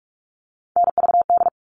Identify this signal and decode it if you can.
Morse code, heard as a keyed tone, about 44 wpm, 700 Hz. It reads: NVD
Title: All Cut Numbers - 3-Digit 45wpm